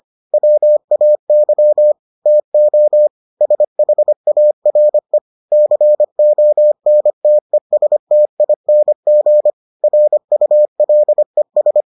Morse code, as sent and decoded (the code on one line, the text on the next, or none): .-- .- -.-- / - --- / ... .... .- .-. / -.-. --- -. - . ... - .. -. --. / .-. ..- .-.. ...
WAY TO SHARE CONTESTING RULES